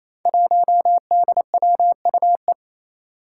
Morse code, heard as a keyed tone, about 28 wpm, 700 Hz. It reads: 1DWUE
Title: Five Challenging Letter-Number Combinations 28wpm